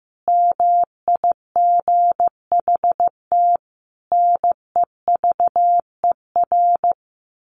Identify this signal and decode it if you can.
Morse code, heard as a keyed tone, about 15 wpm, 700 Hz. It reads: MIGHT NEVER
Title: A Sound Of Thunder 15wpm